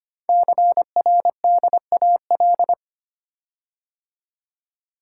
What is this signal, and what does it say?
Morse code, heard as a keyed tone, about 25 wpm, 700 Hz. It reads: CRDAL
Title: Code Groups - 5 Letters - 25wpm